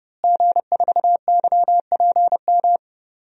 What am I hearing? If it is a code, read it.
Morse code, heard as a keyed tone, about 32 wpm, 700 Hz. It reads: G4YPM
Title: DXCC-100 Call Signs - 5 Characters - Repeated 2x - 30wpm